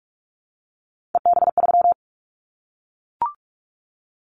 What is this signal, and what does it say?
Morse code, heard as a keyed tone, about 44 wpm, 700 Hz. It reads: EB3
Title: Callsign Prefixes for Top 25 DXCC - Repeated 3x - 45wpm